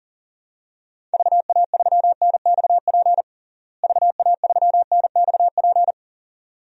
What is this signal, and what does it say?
Morse code, heard as a keyed tone, about 40 wpm, 700 Hz. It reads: VA3NXP VA3NXP
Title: DXCC-100 Call Signs - 6 Characters - Repeated 2x - 40wpm